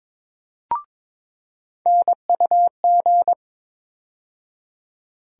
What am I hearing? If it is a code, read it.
Morse code, heard as a keyed tone, about 22 wpm, 700 Hz. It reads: NUG
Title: USA Call Sign Postfixes - 3 Letters 22wpm